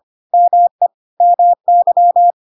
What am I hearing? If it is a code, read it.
Morse code, heard as a keyed tone, about 25 wpm, 700 Hz. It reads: ME MY
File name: Pinocchio / Pinocchio ch3 25 25 ik1hge